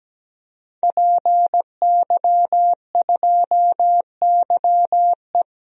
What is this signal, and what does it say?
Morse code, heard as a keyed tone, about 17 wpm, 700 Hz. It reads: PY2YE
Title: DXCC-100 Call Signs - 5 Characters - Repeated 2x - 17wpm